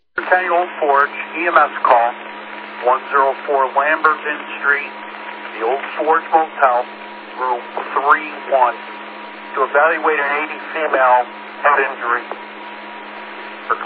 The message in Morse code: none

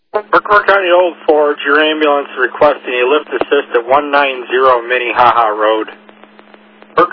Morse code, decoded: none

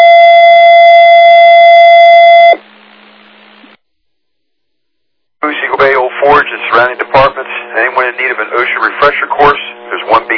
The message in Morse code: none

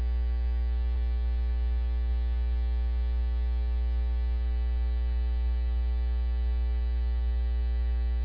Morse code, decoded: none

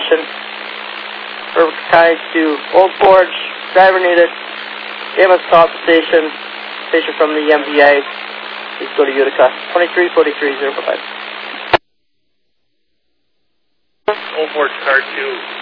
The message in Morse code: none

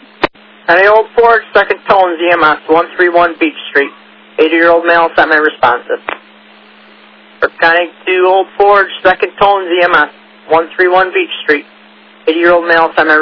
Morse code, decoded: none

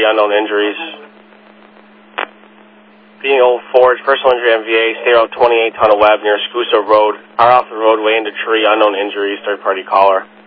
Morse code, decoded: none

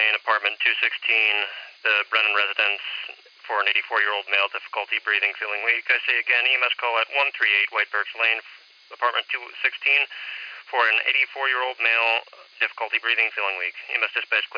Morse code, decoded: none

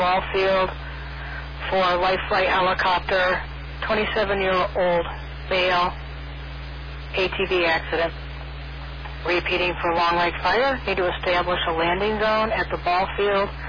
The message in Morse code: none

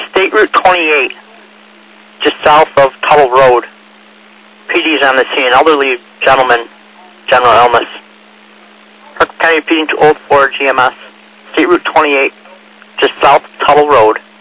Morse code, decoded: none

